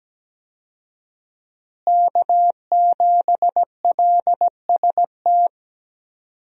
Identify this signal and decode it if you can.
Morse code, heard as a keyed tone, about 17 wpm, 700 Hz. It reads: K7LST
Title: USA Call Signs - 5 Characters - Repeated 2x - 17wpm